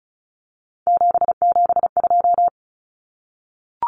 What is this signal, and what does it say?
Morse code, heard as a keyed tone, about 34 wpm, 700 Hz. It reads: 772
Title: Numbers - 3-Digit 35wpm